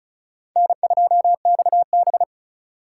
Morse code, heard as a keyed tone, about 36 wpm, 700 Hz. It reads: N2XB
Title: USA Call Signs - 3x Character Spacing - 35wpm